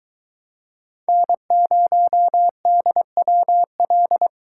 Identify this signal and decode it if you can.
Morse code, heard as a keyed tone, about 23 wpm, 700 Hz. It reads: N0DWL